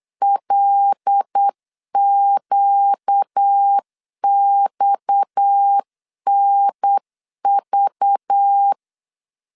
Morse code, heard as a keyed tone, about 8 wpm, 800 Hz. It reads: LQXNV
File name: LEZIONI / 7 GF9Z / 7 G 8wpm 3118